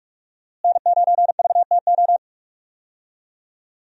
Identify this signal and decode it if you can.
Morse code, heard as a keyed tone, about 46 wpm, 700 Hz. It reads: N9VTO
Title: USA Call Signs - 5 Characters - Repeated 3x - 45wpm